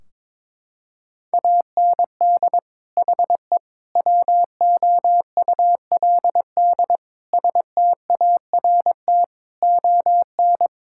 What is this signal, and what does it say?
Morse code, heard as a keyed tone, about 22 wpm, 700 Hz. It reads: AND HE WOULD START ON
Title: Sentences from Top 1600 Words 22wpm